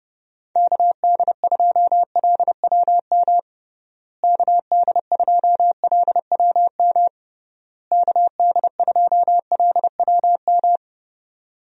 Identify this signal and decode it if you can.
Morse code, heard as a keyed tone, about 31 wpm, 700 Hz. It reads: KD2LWM KD2LWM KD2LWM
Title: USA Call Signs - 6 Characters - Repeated 3x - 30wpm